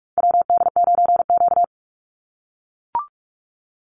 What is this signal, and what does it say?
Morse code, heard as a keyed tone, about 44 wpm, 700 Hz. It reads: WD9Q
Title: Call Signs - 45wpm